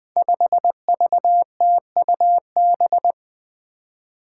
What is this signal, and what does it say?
Morse code, heard as a keyed tone, about 20 wpm, 700 Hz. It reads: 5VTUB
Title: Five Challenging Letter-Number Combinations 20wpm